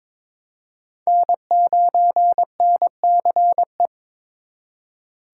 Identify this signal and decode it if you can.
Morse code, heard as a keyed tone, about 26 wpm, 700 Hz. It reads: N9NCE
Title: Speed Racing - USA 1x3 Call Signs 22wpm